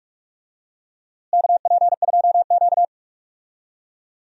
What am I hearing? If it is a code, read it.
Morse code, heard as a keyed tone, about 46 wpm, 700 Hz. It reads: KP2Q
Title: DXCC-100 Call Signs - 4 Characters - Repeated 2x - 45wpm